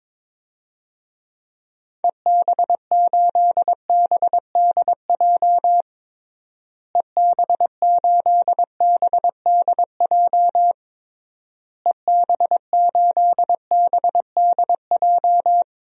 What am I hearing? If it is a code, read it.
Morse code, heard as a keyed tone, about 21 wpm, 700 Hz. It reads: EB8BDJ EB8BDJ EB8BDJ
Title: DXCC-100 Call Signs - 6 Characters - Repeated 3x - 22wpm